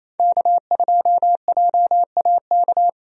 Morse code, heard as a keyed tone, about 28 wpm, 700 Hz. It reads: K2JAK
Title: USA Call Signs - 5 Characters - Repeated 2x - 28wpm